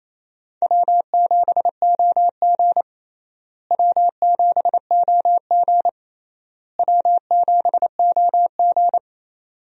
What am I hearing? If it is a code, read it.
Morse code, heard as a keyed tone, about 28 wpm, 700 Hz. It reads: W7OG W7OG W7OG
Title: USA Call Signs - 4 Characters - Repeated 3x - 28wpm